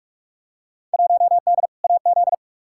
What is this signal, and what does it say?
Morse code, heard as a keyed tone, about 42 wpm, 700 Hz. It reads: 1D AZ
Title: Field Day Exchanges - 45wpm